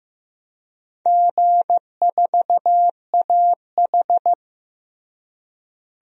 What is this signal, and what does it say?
Morse code, heard as a keyed tone, about 15 wpm, 700 Hz. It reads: G4AH
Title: DXCC-100 Call Signs - 4 Characters - Repeated 3x - 15wpm